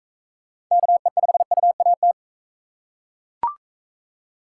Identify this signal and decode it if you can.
Morse code, heard as a keyed tone, about 38 wpm, 700 Hz. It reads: KE5UAT